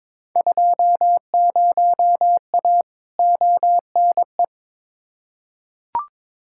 Morse code, heard as a keyed tone, about 22 wpm, 700 Hz. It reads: 20A ONE
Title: Field Day Exchanges Encore - 22wpm